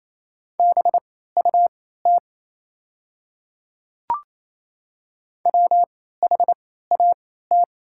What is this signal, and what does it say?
Morse code, heard as a keyed tone, about 28 wpm, spaced as for 16 wpm, 700 Hz. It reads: BUT WHAT
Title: 3x Character Spacing - Top 100 Words - 28wpm